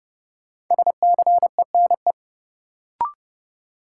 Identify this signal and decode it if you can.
Morse code, heard as a keyed tone, about 30 wpm, 700 Hz. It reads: SCENE
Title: Sentences from Top 1100 Words - New Word Drill - 30wpm